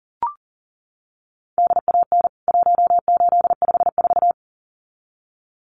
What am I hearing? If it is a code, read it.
Morse code, heard as a keyed tone, about 40 wpm, 700 Hz. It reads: DAN 1854
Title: CWT Exchange 40wpm